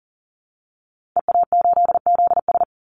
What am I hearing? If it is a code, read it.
Morse code, heard as a keyed tone, about 40 wpm, 700 Hz. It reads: EA8ZS
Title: Call Signs - 40wpm